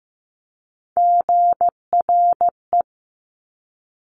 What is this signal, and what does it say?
Morse code, heard as a keyed tone, about 15 wpm, 700 Hz. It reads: GRE
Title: Top Three Letter Combinations in Words 15wpm